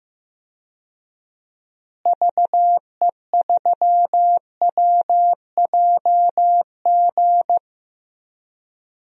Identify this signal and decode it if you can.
Morse code, heard as a keyed tone, about 15 wpm, 700 Hz. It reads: VE3WJG
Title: DXCC-100 Call Signs - 6 Characters - Repeated 2x - 15wpm